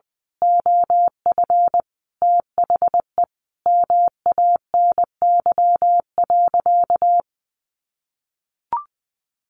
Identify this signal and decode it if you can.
Morse code, heard as a keyed tone, about 20 wpm, 700 Hz. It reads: OF THE MANY.
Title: Sentences from Top 1300 Words 20wpm